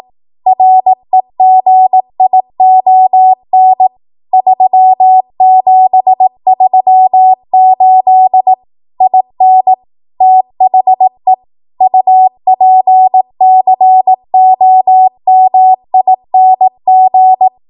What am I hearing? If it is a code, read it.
Morse code, heard as a keyed tone, about 18 wpm, 750 Hz. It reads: REGION 3738 IN THE UPCOMING
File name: Archive / 18 WPM / 18WPM